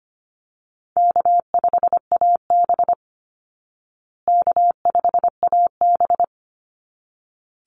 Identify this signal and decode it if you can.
Morse code, heard as a keyed tone, about 25 wpm, 700 Hz. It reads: K5AB K5AB
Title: Call Signs Repeated 3x - 25wpm